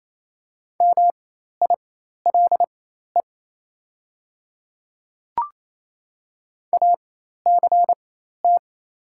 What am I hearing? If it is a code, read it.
Morse code, heard as a keyed tone, about 28 wpm, spaced as for 13 wpm, 700 Hz. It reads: MILE ACT
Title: 4x Character Spacing - Top 300 Words - 28wpm